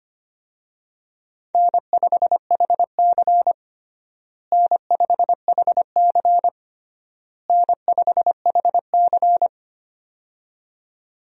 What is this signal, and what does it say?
Morse code, heard as a keyed tone, about 25 wpm, 700 Hz. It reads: N5HC N5HC N5HC
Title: USA Call Signs - 4 Characters - Repeated 3x - 25wpm